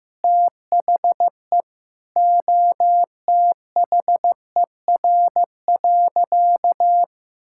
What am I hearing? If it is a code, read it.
Morse code, heard as a keyed tone, about 15 wpm, 700 Hz. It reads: THE OTHER.
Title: Sentences from Top 1000 Words - Review Part 5 - 15wpm